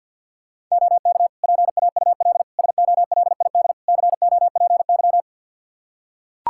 Morse code, heard as a keyed tone, about 45 wpm, 700 Hz. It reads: OK PAUL SOLID COPY
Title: K1USN SST QSO Practice 50wpm